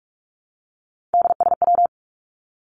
Morse code, heard as a keyed tone, about 44 wpm, 700 Hz. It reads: DSW
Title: Abbreviations Repeated 3x - 45wpm